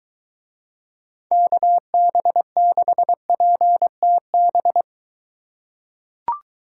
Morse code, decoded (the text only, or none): KB6PTB